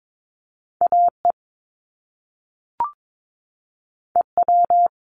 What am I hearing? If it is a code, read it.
Morse code, heard as a keyed tone, about 22 wpm, 700 Hz. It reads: AE EW
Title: Top Two Letter Combinations in Words 22wpm